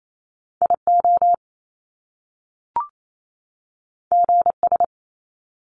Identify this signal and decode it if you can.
Morse code, heard as a keyed tone, about 28 wpm, 700 Hz. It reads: IO GS
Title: Top Two Letter Combinations in Words 28wpm